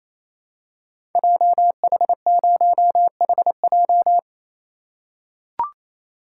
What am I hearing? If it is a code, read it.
Morse code, heard as a keyed tone, about 27 wpm, 700 Hz. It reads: JH0HJ
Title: Code Groups - 5 Letter:Numbers - 28wpm